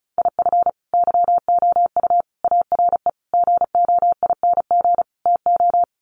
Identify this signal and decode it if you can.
Morse code, heard as a keyed tone, about 34 wpm, 700 Hz. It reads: IF YOU ARE GOING TO
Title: Sets of 5 Words Encore2 35wpm